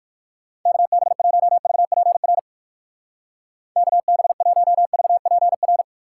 Morse code, heard as a keyed tone, about 58 wpm, 700 Hz. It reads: KB1VPR KB1VPR